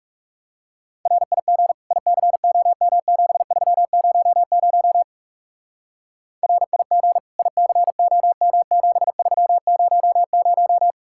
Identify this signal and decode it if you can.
Morse code, heard as a keyed tone, about 39 wpm, 700 Hz. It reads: RIG ICOM7300 RIG ICOM7300